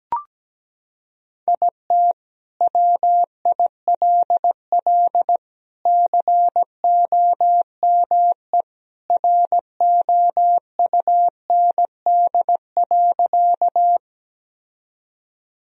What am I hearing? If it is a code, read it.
Morse code, heard as a keyed tone, about 17 wpm, 700 Hz. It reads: IT WILL COME ROUND.
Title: Sentences from Top 1000 Words - Review Part 1 - 17wpm